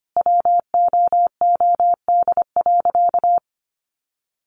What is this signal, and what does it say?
Morse code, heard as a keyed tone, about 25 wpm, 700 Hz. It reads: WOOD.
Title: Sentences from Top 1100 Words 25wpm